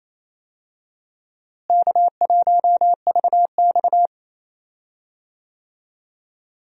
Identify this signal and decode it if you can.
Morse code, heard as a keyed tone, about 27 wpm, 700 Hz. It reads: K1VX
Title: USA Call Signs - 4 Characters - Repeated 3x - 28wpm